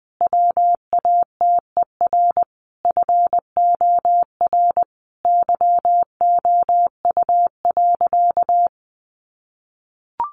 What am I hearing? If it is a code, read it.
Morse code, heard as a keyed tone, about 20 wpm, 700 Hz. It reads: WATER FOR YOU.
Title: Sentences from Top 1300 Words 20wpm